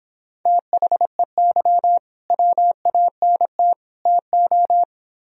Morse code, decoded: THEY WANT TO

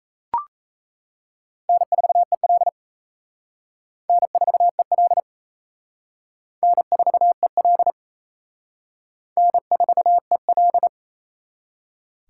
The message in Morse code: N4EL N4EL N4EL N4EL

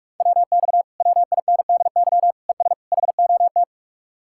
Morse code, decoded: WX WINDY ES HOT

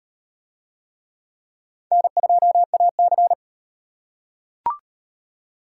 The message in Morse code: N2AC

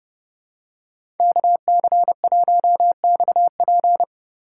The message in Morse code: KC1XP

